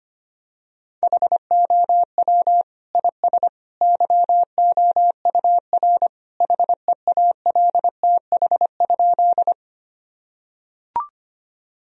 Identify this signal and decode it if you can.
Morse code, heard as a keyed tone, about 25 wpm, 700 Hz. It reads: HOW IS YOUR HEALTH?
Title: Sentences from Top 1000 Words - Review Part 3 - 25wpm